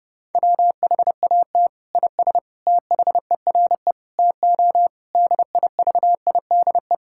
Morse code, heard as a keyed tone, about 30 wpm, 700 Hz. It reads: WHAT IS THERE TO DIVIDE
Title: Sentences from Top 1600 Words 30wpm